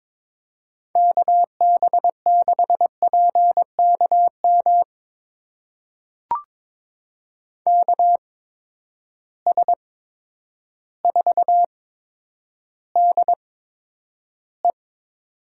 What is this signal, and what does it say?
Morse code, heard as a keyed tone, about 22 wpm, spaced as for 10 wpm, 700 Hz. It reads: KB6PKM KS4DE
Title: USA Call Signs - 8x Character Spacing - 22wpm